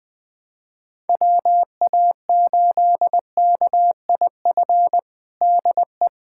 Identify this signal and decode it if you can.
Morse code, heard as a keyed tone, about 20 wpm, 700 Hz. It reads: WA8KIF DE
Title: USA Callsign DE Callsign - 6 Characters - 20wpm